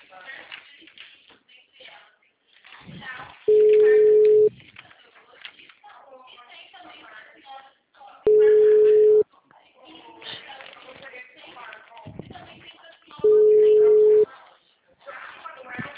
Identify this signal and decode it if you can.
Ringback tone (Europe, 425 Hz), with speech.